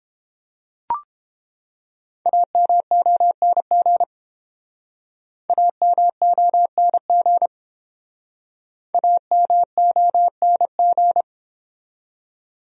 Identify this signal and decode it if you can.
Morse code, heard as a keyed tone, about 29 wpm, 700 Hz. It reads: AMONG AMONG AMONG